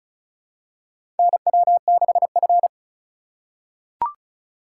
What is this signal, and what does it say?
Morse code, heard as a keyed tone, about 36 wpm, 700 Hz. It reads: NW6F
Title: USA Call Signs - 4 Characters - 4x Character Spacing - 35wpm